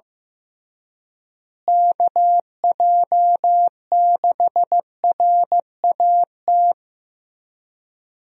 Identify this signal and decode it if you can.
Morse code, heard as a keyed tone, about 15 wpm, 700 Hz. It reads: KJ6RAT
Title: USA Call Signs - 6 Characters - Repeated 3x - 15wpm